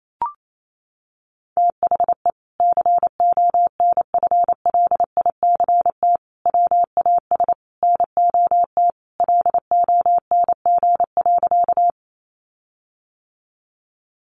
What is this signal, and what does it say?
Morse code, heard as a keyed tone, about 28 wpm, 700 Hz. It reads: THE CONFLICT WAS NOT LONG.
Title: Sentences from Top 1300 Words 28wpm